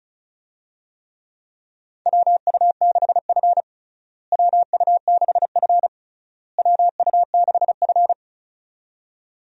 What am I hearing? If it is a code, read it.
Morse code, heard as a keyed tone, about 36 wpm, 700 Hz. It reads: WU6F WU6F WU6F
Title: USA Call Signs - 4 Characters - Repeated 3x - 35wpm